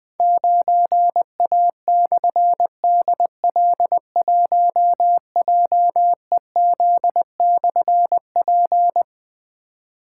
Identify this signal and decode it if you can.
Morse code, heard as a keyed tone, about 20 wpm, 700 Hz. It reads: 9A/DL1JEZ/P
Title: Difficult Call Signs 20wpm